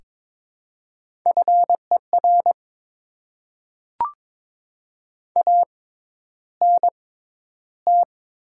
Morse code, heard as a keyed tone, about 22 wpm, spaced as for 11 wpm, 690 Hz. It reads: FER ANT
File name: Top 20 QSO Elements - 6x Character Spacing - 22wpm